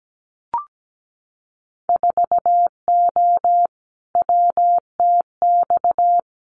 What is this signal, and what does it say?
Morse code, heard as a keyed tone, about 17 wpm, 700 Hz. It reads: 4O WTX
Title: Winter Field Day Exchanges Encore 17wpm